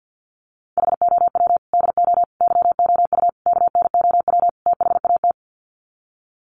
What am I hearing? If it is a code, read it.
Morse code, heard as a keyed tone, about 50 wpm, 700 Hz. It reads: HOW DO YOU KNOW THAT
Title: Sets of 5 Words Encore 50wpm